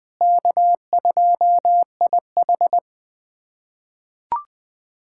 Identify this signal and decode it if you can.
Morse code, heard as a keyed tone, about 20 wpm, 700 Hz. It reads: K2IH
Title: USA Callsign DE Callsign - 4 Characters - 20wpm